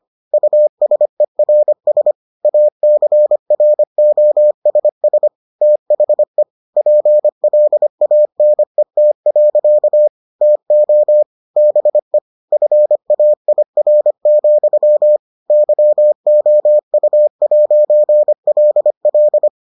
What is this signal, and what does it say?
Morse code, heard as a keyed tone, about 25 wpm, 600 Hz. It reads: USERS ACROSS THE PLANET. TO BE FAIR, YOU'LL